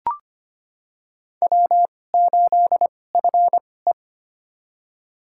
Morse code, heard as a keyed tone, about 25 wpm, spaced as for 18 wpm, 700 Hz. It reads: W8FE
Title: USA Call Signs - 4 Characters - 2x Character Spacing - 25wpm